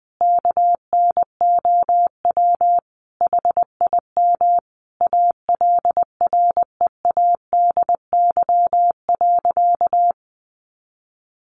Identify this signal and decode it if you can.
Morse code, heard as a keyed tone, about 20 wpm, 700 Hz. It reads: KNOW HIM ALREADY.
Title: Sentences from Top 1000 Words 20wpm